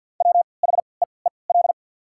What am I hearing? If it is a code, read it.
Morse code, heard as a keyed tone, about 51 wpm, spaced as for 28 wpm, 690 Hz. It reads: WHEEL
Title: 3x Character Spacing - Top 500 Words - 50wpm